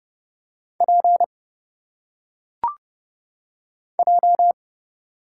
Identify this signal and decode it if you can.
Morse code, heard as a keyed tone, about 30 wpm, 700 Hz. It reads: P J
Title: Single Letters - Phonetic Pronunciation 30wpm